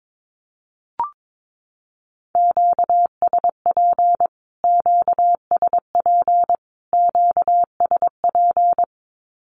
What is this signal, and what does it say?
Morse code, heard as a keyed tone, about 22 wpm, 700 Hz. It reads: QSP QSP QSP
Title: Q-Codes Repeated 3x 22wpm